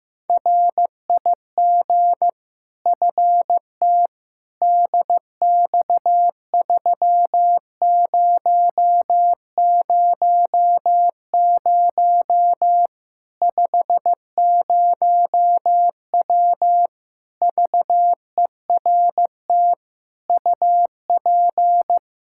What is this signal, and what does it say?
Morse code, heard as a keyed tone, about 15 wpm, 700 Hz. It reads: RIG FT DX3000 50W VERT UP
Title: Practice QSO - Exchange 2 - Info - 15wpm